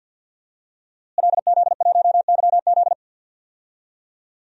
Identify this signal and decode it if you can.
Morse code, heard as a keyed tone, about 51 wpm, 700 Hz. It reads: LZ1YZ
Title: DXCC-100 Call Signs - 5 Characters - Repeated 2x - 50wpm